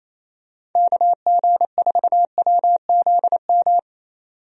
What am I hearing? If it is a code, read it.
Morse code, heard as a keyed tone, about 28 wpm, 700 Hz. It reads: KG4WZM
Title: USA Call Signs - 6 Characters - 28wpm